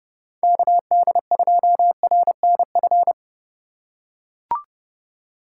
Morse code, heard as a keyed tone, about 30 wpm, 700 Hz. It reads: KD2RNF
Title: USA Call Signs - 6 Characters - 3x Character Spacing - 30wpm